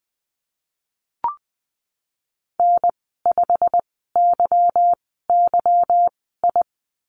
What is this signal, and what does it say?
Morse code, heard as a keyed tone, about 20 wpm, spaced as for 15 wpm, 700 Hz. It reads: N5YYI
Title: 154 Lesson 25g - Letter p - Call Signs 20wpm